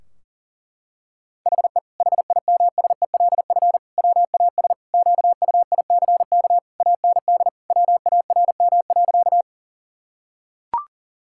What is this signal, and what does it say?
Morse code, heard as a keyed tone, about 37 wpm, 700 Hz. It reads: HE HIMSELF WAS QUICK AND WARM.
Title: Sentences from Top 1600 Words 40wpm